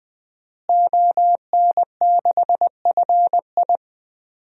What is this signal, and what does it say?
Morse code, heard as a keyed tone, about 21 wpm, 700 Hz. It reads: ON6FI
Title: DXCC-100 Call Signs - 5 Characters - Repeated 3x - 20wpm